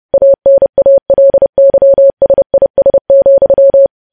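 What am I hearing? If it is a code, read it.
Morse code, heard as a keyed tone, about 30 wpm, 550 Hz. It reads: ANALYSIS,
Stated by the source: Morse news headlines sent at 30.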